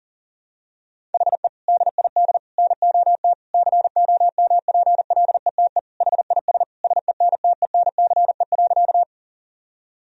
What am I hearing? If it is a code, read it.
Morse code, heard as a keyed tone, about 40 wpm, 700 Hz. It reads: HE DID NOT COMPLETE HIS SENTENCE.
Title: Sentences from Top 1000 Words - Review Part 1 - 40wpm